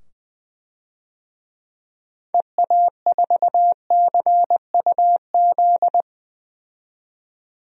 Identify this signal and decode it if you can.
Morse code, heard as a keyed tone, about 20 wpm, 700 Hz. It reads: EA4CUZ
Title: DXCC-100 Call Signs - 6 Characters - 20wpm